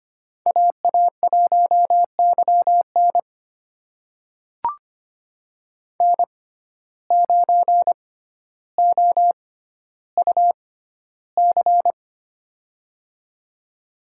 Morse code, heard as a keyed tone, about 25 wpm, spaced as for 13 wpm, 700 Hz. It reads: AA1YN N9OUC
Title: USA Call Signs - 5 Characters - 6x Character Spacing - 25wpm